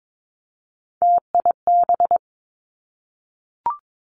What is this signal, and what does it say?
Morse code, heard as a keyed tone, about 22 wpm, 700 Hz. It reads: TIB
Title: Three Challenging Letter-Number Combinations 22wpm